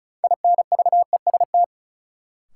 Morse code, heard as a keyed tone, about 34 wpm, 700 Hz. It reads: INVEST
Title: Sentences from Top 1100 Words - New Word Drill Alt - 35wpm